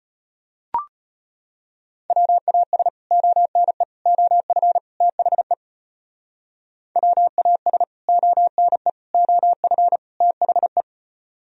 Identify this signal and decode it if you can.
Morse code, heard as a keyed tone, about 33 wpm, 700 Hz. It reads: WAS ONE OF THE WAS ONE OF THE